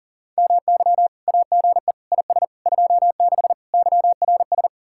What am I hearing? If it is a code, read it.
Morse code, heard as a keyed tone, about 50 wpm, 700 Hz. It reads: MY AGE IS 26 YRS